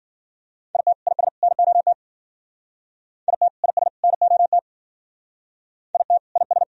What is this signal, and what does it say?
Morse code, heard as a keyed tone, about 51 wpm, 700 Hz. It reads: IT IS NOT IT IS NOT IT IS